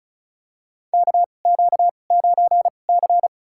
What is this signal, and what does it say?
Morse code, heard as a keyed tone, about 36 wpm, spaced as for 25 wpm, 700 Hz. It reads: KQ9C
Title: USA Call Signs - 4 Characters - 2x Character Spacing - 35wpm